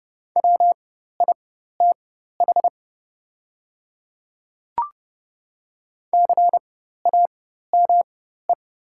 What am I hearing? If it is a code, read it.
Morse code, heard as a keyed tone, about 30 wpm, spaced as for 14 wpm, 700 Hz. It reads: WITH CAME
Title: 4x Character Spacing - Top 200 Words - 30wpm